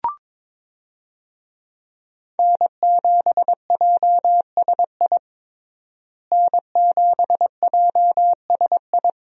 Morse code, heard as a keyed tone, about 22 wpm, 700 Hz. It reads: N7JSI N7JSI
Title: USA Call Signs - 5 Characters - Repeated 2x - 22wpm